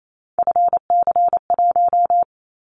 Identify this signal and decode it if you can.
Morse code, heard as a keyed tone, about 28 wpm, 700 Hz. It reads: FC1
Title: Callsign Prefixes for Top 50 DXCC - 28wpm